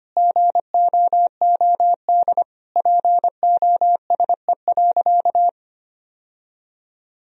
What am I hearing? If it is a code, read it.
Morse code, heard as a keyed tone, about 25 wpm, 700 Hz. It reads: GOOD POSE.
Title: Sentences from Top 1000 Words - Review Part 2 - 25wpm